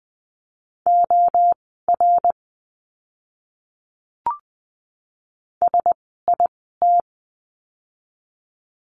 Morse code, heard as a keyed tone, about 20 wpm, spaced as for 14 wpm, 700 Hz. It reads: OR SIT
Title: 108 Lesson 18f - Letter m - Words 20wpm